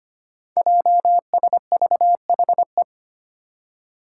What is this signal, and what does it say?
Morse code, heard as a keyed tone, about 25 wpm, 700 Hz. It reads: JSVHE
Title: Five Challenging Letter-Number Combinations 25wpm